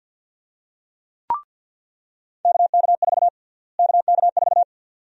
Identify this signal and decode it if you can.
Morse code, heard as a keyed tone, about 46 wpm, 700 Hz. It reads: KK4 KK4
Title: USA Call Sign Prefixes - Repeated 2x - 50wpm